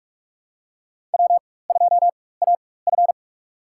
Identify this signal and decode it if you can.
Morse code, heard as a keyed tone, about 46 wpm, spaced as for 21 wpm, 700 Hz. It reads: W2AF
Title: USA Call Signs - 4 Characters - 4x Character Spacing - 45wpm